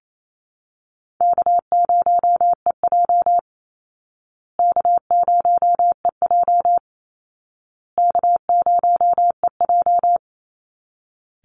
Morse code, heard as a keyed tone, about 28 wpm, 700 Hz. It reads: K0EJ K0EJ K0EJ
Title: Easy Call Signs Repeated 3x - 28wpm